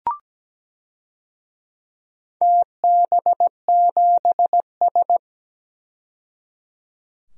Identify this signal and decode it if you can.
Morse code, heard as a keyed tone, about 17 wpm, 700 Hz. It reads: TB7S
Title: DXCC-100 Call Signs - 4 Characters - 17wpm